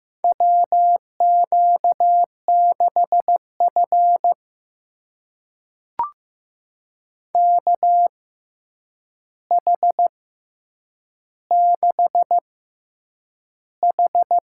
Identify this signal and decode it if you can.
Morse code, heard as a keyed tone, about 15 wpm, spaced as for 8 wpm, 700 Hz. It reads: WQ6F KH6H
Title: USA Call Signs - 4 Characters - 6x Character Spacing - 15wpm